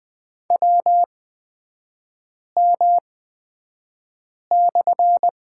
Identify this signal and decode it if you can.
Morse code, heard as a keyed tone, about 20 wpm, spaced as for 5 wpm, 700 Hz. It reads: WM/
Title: 250 Lesson 40e - BK - Three Characters 20wpm